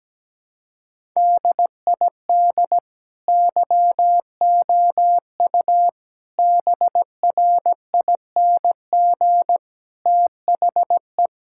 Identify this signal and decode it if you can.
Morse code, heard as a keyed tone, about 17 wpm, 700 Hz. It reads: DID YOU BRING THE
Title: Sentences from Top 1000 Words - Review Part 6 - 17wpm